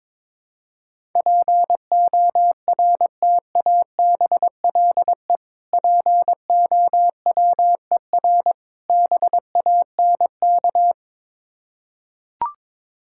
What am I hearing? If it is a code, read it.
Morse code, heard as a keyed tone, about 22 wpm, 690 Hz. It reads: PORTABLE POWER BANK
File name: Difficult Words with Hints - 22wpm